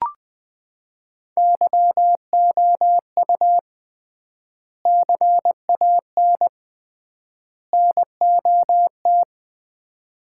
Morse code, heard as a keyed tone, about 20 wpm, 700 Hz. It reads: YOU CAN NOT
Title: 3x Word-Spacing - Sets of 3 Words - 20wpm